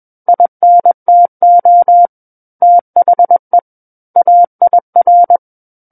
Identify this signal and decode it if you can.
Morse code, heard as a keyed tone, about 21 wpm, 700 Hz. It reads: INTO THE AIR
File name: Book One / 12-Chapter 12 - 21 WPM